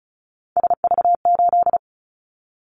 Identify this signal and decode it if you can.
Morse code, heard as a keyed tone, about 36 wpm, 700 Hz. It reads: SV8
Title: Callsign Prefixes for Top 50 DXCC - 35wpm